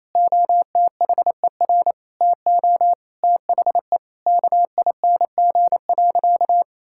Morse code, heard as a keyed tone, about 28 wpm, 700 Hz. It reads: OTHER TO THE KING.
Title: Sentences from Top 1000 Words - Review Part 6 - 28wpm